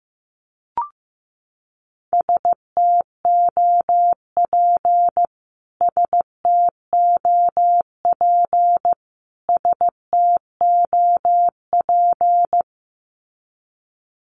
Morse code, heard as a keyed tone, about 15 wpm, 700 Hz. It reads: STOP STOP STOP
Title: Top 500 Words Repeated 15wpm